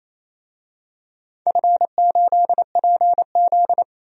Morse code, heard as a keyed tone, about 28 wpm, 700 Hz. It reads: F8PZ
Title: DXCC-100 Call Signs - 4 Characters - 28wpm